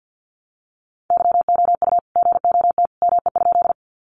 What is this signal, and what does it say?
Morse code, heard as a keyed tone, about 50 wpm, 700 Hz. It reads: YOU GOT ME?
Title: Sentences from Top 300 Words 50wpm